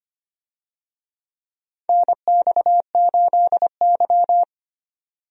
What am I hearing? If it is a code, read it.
Morse code, heard as a keyed tone, about 25 wpm, 700 Hz. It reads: NX8Y
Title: USA Call Signs - 4 Characters - 25wpm